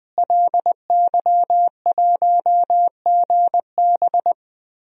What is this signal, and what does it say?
Morse code, heard as a keyed tone, about 21 wpm, 700 Hz. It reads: LY1GB
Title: DXCC-100 Call Signs - 5 Characters - Repeated 3x - 20wpm